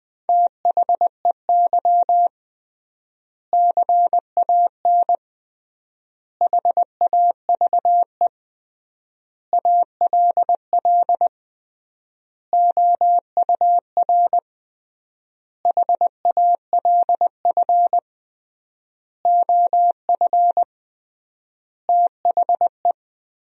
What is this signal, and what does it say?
Morse code, heard as a keyed tone, about 20 wpm, 700 Hz. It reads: THEY CAN HAVE ALL OUR HALF OF THE
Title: 3x Word-Spacing - Sentences from Top 400 Words - 20wpm